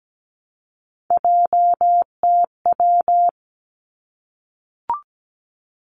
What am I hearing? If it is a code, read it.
Morse code, heard as a keyed tone, about 17 wpm, 700 Hz. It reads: JTW
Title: Three Challenging Letter-Number Combinations 17wpm